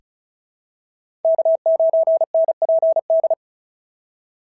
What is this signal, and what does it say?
Morse code, heard as a keyed tone, about 36 wpm, 690 Hz. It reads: K9NPD
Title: SOTA QSO Practice 35wpm